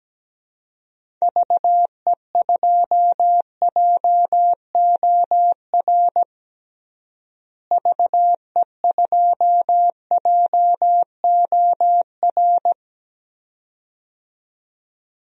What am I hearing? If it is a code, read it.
Morse code, heard as a keyed tone, about 17 wpm, 700 Hz. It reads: VE2JOR VE2JOR
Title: DXCC-100 Call Signs - 6 Characters - Repeated 2x - 17wpm